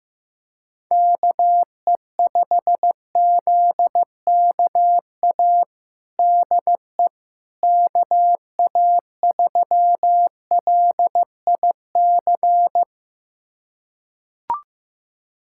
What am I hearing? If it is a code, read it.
Morse code, heard as a keyed tone, about 15 wpm, 700 Hz. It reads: KE5ZKA DE KA3LIC
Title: USA Callsign DE Callsign 15wpm